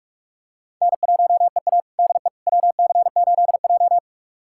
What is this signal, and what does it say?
Morse code, heard as a keyed tone, about 41 wpm, 700 Hz. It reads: N1EA DE WC8J
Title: USA Callsign DE Callsign - 4 Characters - 45wpm